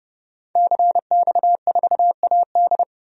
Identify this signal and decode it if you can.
Morse code, heard as a keyed tone, about 30 wpm, 700 Hz. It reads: CX4AD
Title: DXCC-100 Call Signs - 5 Characters - 30wpm